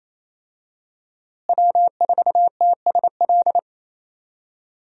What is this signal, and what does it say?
Morse code, heard as a keyed tone, about 28 wpm, 700 Hz. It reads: W4TSL
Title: USA Call Signs - 5 Characters - Repeated 2x - 28wpm